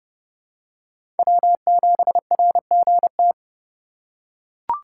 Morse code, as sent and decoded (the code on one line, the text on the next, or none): .-- --... .-. --. -
W7RGT